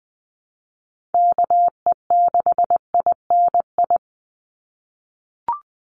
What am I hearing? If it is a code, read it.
Morse code, heard as a keyed tone, about 20 wpm, 700 Hz. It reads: KE6INI
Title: USA Call Signs 20wpm